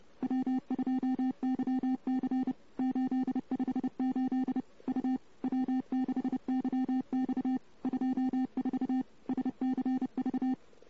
Morse code, heard as a keyed tone, about 29 wpm, 260 Hz. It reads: W2YC 858 U W6YX 24 SCV
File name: CW / 2008-ss-cw-k5tr-op-n5rz -14:00